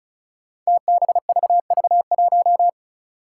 Morse code, heard as a keyed tone, about 34 wpm, 700 Hz. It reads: TBVV1
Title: Five Challenging Letter-Number Combinations 35wpm